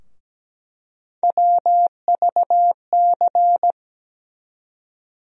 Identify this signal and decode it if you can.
Morse code, heard as a keyed tone, about 17 wpm, 700 Hz. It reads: WVC
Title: USA Call Sign Postfixes - 3 Letters 17wpm